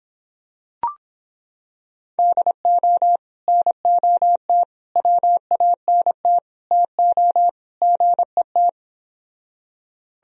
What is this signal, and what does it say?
Morse code, heard as a keyed tone, about 26 wpm, 700 Hz. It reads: DO NOT WANT TO GET